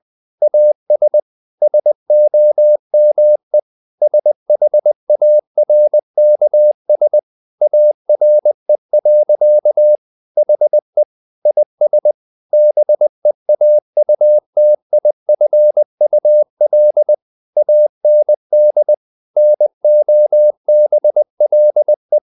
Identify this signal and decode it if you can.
Morse code, heard as a keyed tone, about 20 wpm, 600 Hz. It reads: AS SOME SHARKS ARE. HE IS BEAUTIFUL AND NOBLE
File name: the old man and the sea part 2 20wpm